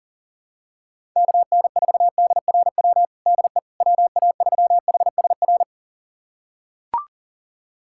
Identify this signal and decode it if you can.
Morse code, heard as a keyed tone, about 40 wpm, 700 Hz. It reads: KN4DRW DE WA3HSR
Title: USA Callsign DE Callsign 40wpm